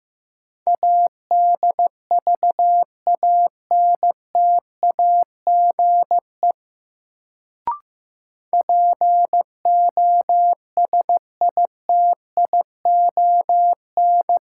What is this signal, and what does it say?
Morse code, heard as a keyed tone, about 15 wpm, 700 Hz. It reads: ADVANTAGE POSITION
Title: Sentences from Top 900 Words - New Word Drill - 15wpm